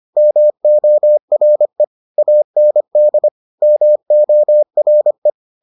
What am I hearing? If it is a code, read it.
Morse code, heard as a keyed tone, about 25 wpm, 600 Hz. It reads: MORE AND MORE